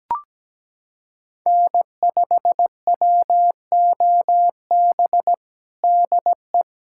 Morse code, heard as a keyed tone, about 17 wpm, 700 Hz. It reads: N5WOB DE
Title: USA Callsign DE Callsign - 5 Characters - 17wpm